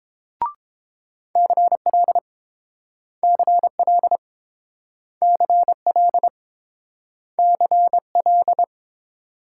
Morse code, heard as a keyed tone, about 29 wpm, 700 Hz. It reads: CL CL CL CL